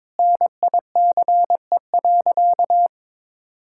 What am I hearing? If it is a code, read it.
Morse code, heard as a keyed tone, about 22 wpm, 700 Hz. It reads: NICE.
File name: practice / 22wpm / Sentences with 3-Word Phrases - 22wpm